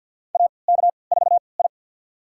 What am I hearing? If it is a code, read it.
Morse code, heard as a keyed tone, about 48 wpm, spaced as for 28 wpm, 700 Hz. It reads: AK4I
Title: USA Call Signs - 4 Characters - 3x Character Spacing - 50wpm